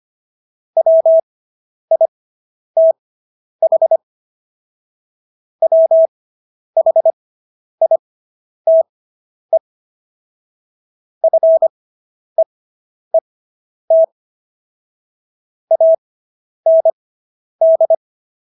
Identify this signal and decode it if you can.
Morse code, heard as a keyed tone, about 25 wpm, spaced as for 10 wpm, 650 Hz. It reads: WITH WHITE FEET AND